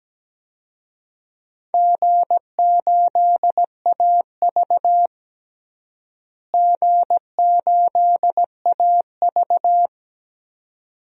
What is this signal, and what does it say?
Morse code, heard as a keyed tone, about 17 wpm, 700 Hz. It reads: G8AV G8AV
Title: DXCC-100 Call Signs - 4 Characters - Repeated 2x - 17wpm